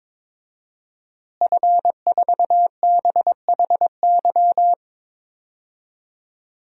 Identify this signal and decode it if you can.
Morse code, heard as a keyed tone, about 22 wpm, 700 Hz. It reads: F4BHY
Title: DXCC-100 Call Signs - 5 Characters - Repeated 3x - 22wpm